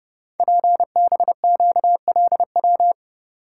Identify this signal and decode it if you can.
Morse code, heard as a keyed tone, about 30 wpm, 700 Hz. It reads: PBQLW
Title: Code Groups - 5 Letters - 30wpm